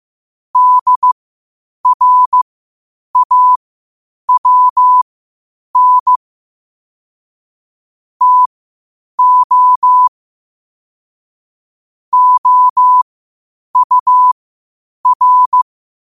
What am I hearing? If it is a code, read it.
Morse code, heard as a keyed tone, about 15 wpm, spaced as for 8 wpm, 1000 Hz. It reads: DRAWN TO OUR
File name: qotdfiles / 8wpm